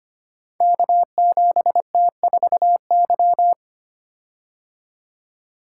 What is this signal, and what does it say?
Morse code, heard as a keyed tone, about 25 wpm, 700 Hz. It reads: K7T4Y
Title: Code Groups - 5 Letter:Numbers - 25wpm